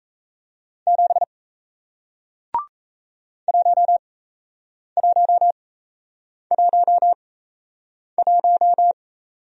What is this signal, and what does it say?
Morse code, heard as a keyed tone, about 37 wpm, spaced as for 8 wpm, 700 Hz. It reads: 7 1111